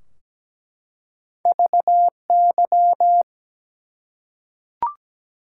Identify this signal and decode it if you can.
Morse code, heard as a keyed tone, about 17 wpm, 700 Hz. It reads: VY
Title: Top 90 QSO Elements 17wpm